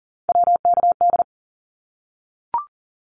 Morse code, heard as a keyed tone, about 42 wpm, 700 Hz. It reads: WKD
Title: Abbreviations 40wpm